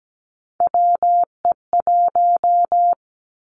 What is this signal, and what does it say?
Morse code, heard as a keyed tone, about 17 wpm, 700 Hz. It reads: WE1
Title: Three Challenging Letter-Number Combinations 17wpm